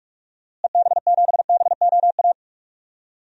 Morse code, heard as a keyed tone, about 46 wpm, 700 Hz. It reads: EB7BOA
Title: DXCC-100 Call Signs - 6 Characters - Repeated 3x - 45wpm